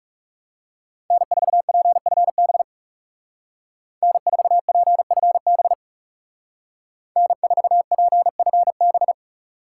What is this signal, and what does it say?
Morse code, heard as a keyed tone, about 40 wpm, 700 Hz. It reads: N4PFB N4PFB N4PFB